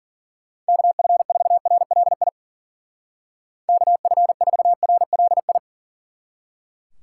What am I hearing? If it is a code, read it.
Morse code, heard as a keyed tone, about 48 wpm, 700 Hz. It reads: KF4RLI KF4RLI
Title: Speed Racing - USA Call Signs 40wpm